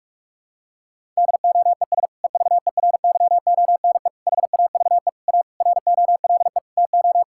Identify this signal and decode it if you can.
Morse code, heard as a keyed tone, about 41 wpm, 700 Hz. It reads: DOES EVERYONE HAVE A ROLE TO
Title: Sentences from Top 1000 Words - Review Part 6 - 45wpm